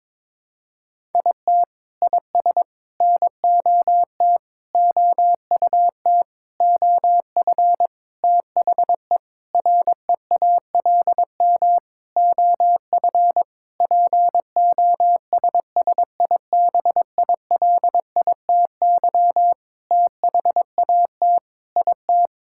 Morse code, heard as a keyed tone, about 22 wpm, 700 Hz. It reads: IT IS NOT OUT OF THE REALM OF POSSIBILITY THAT IT
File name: Sentences with 9-Word Phrases - 22wpm